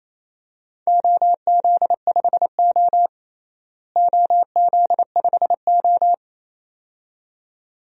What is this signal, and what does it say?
Morse code, heard as a keyed tone, about 28 wpm, 700 Hz. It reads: OZ5O OZ5O
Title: DXCC-100 Call Signs - 4 Characters - Repeated 2x - 28wpm